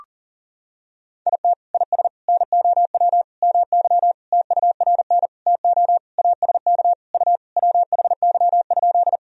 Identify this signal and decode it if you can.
Morse code, heard as a keyed tone, about 40 wpm, 700 Hz. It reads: IT IS NOW MY TURN TO ASK U WHY?
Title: Sentences from Top 200 Words and Abbreviations 40wpm